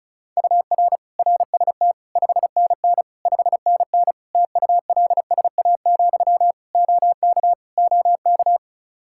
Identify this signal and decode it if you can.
Morse code, heard as a keyed tone, about 36 wpm, 700 Hz. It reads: UR RST 5NN 5NN TULSA, OK OK
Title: Practice QSO - Exchange 1 - Beginning - 35wpm